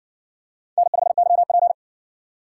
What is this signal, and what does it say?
Morse code, heard as a keyed tone, about 60 wpm, 700 Hz. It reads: N5YP